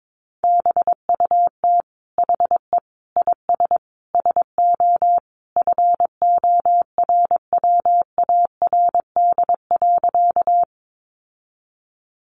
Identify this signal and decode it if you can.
Morse code, heard as a keyed tone, about 22 wpm, 700 Hz. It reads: BUT HE IS SO FORWARD.
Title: Sentences from Top 1300 Words 22wpm